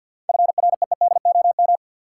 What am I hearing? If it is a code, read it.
Morse code, heard as a keyed tone, about 46 wpm, 700 Hz. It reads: FREEDOM
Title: Seven Letter Words - 50wpm